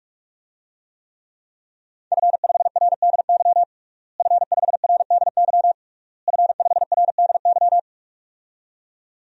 Morse code, heard as a keyed tone, about 41 wpm, 700 Hz. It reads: F5RDY F5RDY F5RDY
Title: DXCC-100 Call Signs - 5 Characters - Repeated 3x - 45wpm